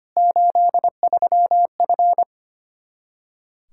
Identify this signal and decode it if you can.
Morse code, heard as a keyed tone, about 25 wpm, 700 Hz. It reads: 83F
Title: Practice QSO - Exchange 2 - Info - 25wpm